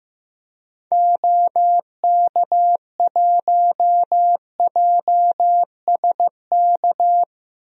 Morse code, heard as a keyed tone, about 15 wpm, 700 Hz. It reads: OK1JSK
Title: DXCC-100 Call Signs - 6 Characters - Repeated 2x - 15wpm